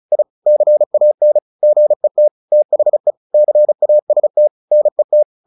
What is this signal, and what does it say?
Morse code, heard as a keyed tone, about 34 wpm, 600 Hz. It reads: I CAN GET THE CAST NET
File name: the old man and the sea / the old man and the sea 1 35wpm